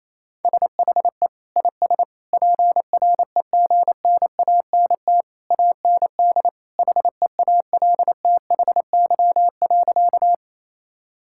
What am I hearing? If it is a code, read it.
Morse code, heard as a keyed tone, about 28 wpm, 700 Hz. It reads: SHE IS PREGNANT AND HEALTHY.
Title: Sentences from Top 1000 Words - Review Part 6 - 28wpm